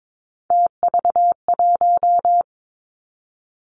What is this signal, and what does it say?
Morse code, heard as a keyed tone, about 22 wpm, 700 Hz. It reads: TV1
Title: Three Challenging Letter-Number Combinations 22wpm